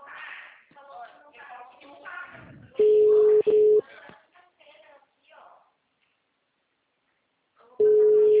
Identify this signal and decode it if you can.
Ringback tone (Europe, 425 Hz), with speech.